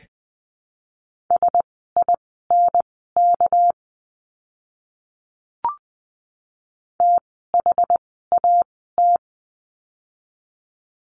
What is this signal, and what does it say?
Morse code, heard as a keyed tone, about 20 wpm, spaced as for 15 wpm, 700 Hz. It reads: SINK THAT
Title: 199 Lesson 32f - Letter k - Words 20wpm